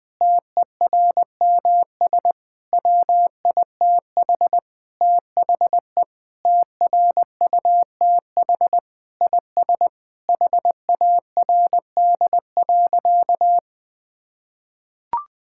Morse code, heard as a keyed tone, about 20 wpm, 700 Hz. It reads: TERMS WITH THE TRUTH IS HARD.
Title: Sentences with 6-Word Phrases - 20wpm